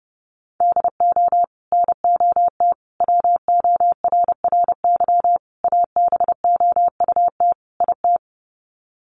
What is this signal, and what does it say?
Morse code, heard as a keyed tone, about 30 wpm, 700 Hz. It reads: DO NOT WORRY ABOUT IT
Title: Sets of 5 Words 30wpm